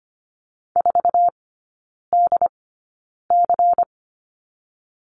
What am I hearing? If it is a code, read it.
Morse code, heard as a keyed tone, about 25 wpm, spaced as for 9 wpm, 700 Hz. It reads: 4DC
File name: course / 25:10wpm / 093 Lesson 16e - Letter c - Three Characters 25wpm